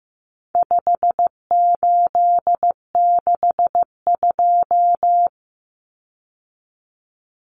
Morse code, heard as a keyed tone, about 15 wpm, 700 Hz. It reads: 5862
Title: Common Cut Numbers - 4-Digit 15wpm